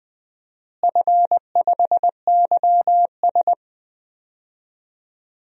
Morse code, heard as a keyed tone, about 20 wpm, 700 Hz. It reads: F5YS
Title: DXCC-100 Call Signs - 4 Characters - Repeated 2x - 20wpm